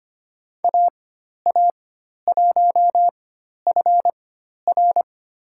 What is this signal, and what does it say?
Morse code, heard as a keyed tone, about 25 wpm, spaced as for 12 wpm, 700 Hz. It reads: AA1FR
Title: USA Call Signs - 5 Characters - 4x Character Spacing - 25wpm